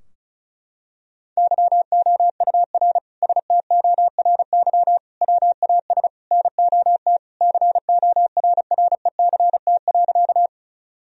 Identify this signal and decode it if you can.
Morse code, heard as a keyed tone, about 36 wpm, 700 Hz. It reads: YOUR STORY WAS NOT CORRECT.
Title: Sentences from Top 1000 Words - Review Part 2 - 35wpm